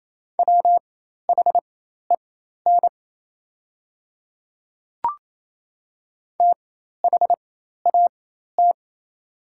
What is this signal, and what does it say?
Morse code, heard as a keyed tone, about 27 wpm, spaced as for 13 wpm, 690 Hz. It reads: WHEN THAT
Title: 4x Character Spacing - Top 100 Words - 28wpm